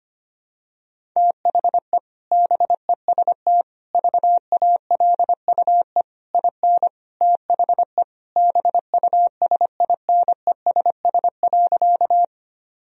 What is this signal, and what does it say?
Morse code, heard as a keyed tone, about 25 wpm, 700 Hz. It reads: THE BEST VALUE IN THE BUSINESS.
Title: Sentences from Top 1000 Words - Review Part 3 - 25wpm